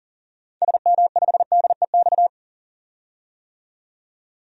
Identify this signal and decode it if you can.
Morse code, heard as a keyed tone, about 40 wpm, 700 Hz. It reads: SM5DEX
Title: DXCC-100 Call Signs - 6 Characters - 40wpm